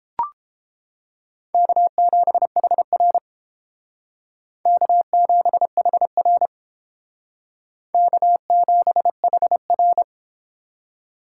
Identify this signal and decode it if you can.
Morse code, heard as a keyed tone, about 31 wpm, 700 Hz. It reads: K7HR K7HR K7HR